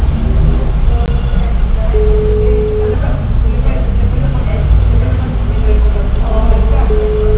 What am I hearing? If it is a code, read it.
Ringback tone (Europe, 425 Hz), with speech.